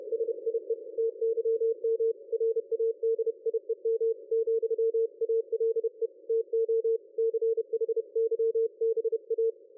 Morse code, heard as a keyed tone, about 30 wpm, 450 Hz. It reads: SIE TYM RADIEM, ALE TO CHYBA
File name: dlazaawansowanych / nagraniazpasma / ExpertSDR2 line out 23.01.2020 14.22.53